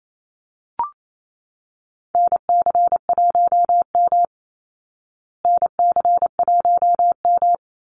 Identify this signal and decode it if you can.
Morse code, heard as a keyed tone, about 28 wpm, 700 Hz. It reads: NC1M NC1M
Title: Call Signs Repeated 2x - 28wpm